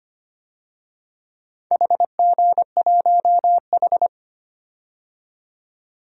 Morse code, heard as a keyed tone, about 25 wpm, 700 Hz. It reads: HG1H
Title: DXCC-100 Call Signs - 4 Characters - 25wpm